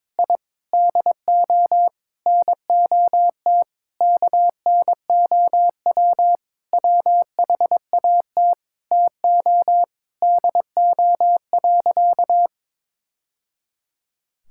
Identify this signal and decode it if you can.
Morse code, heard as a keyed tone, about 22 wpm, 700 Hz. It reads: I DO NOT KNOW WHAT TO DO.
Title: Sentences from Top 1000 Words - Review Part 5 - 22wpm